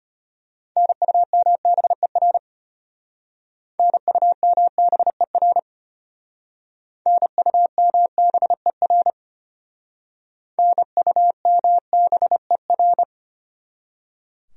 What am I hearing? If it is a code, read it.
Morse code, heard as a keyed tone, about 32 wpm, 700 Hz. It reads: NUMBER NUMBER NUMBER NUMBER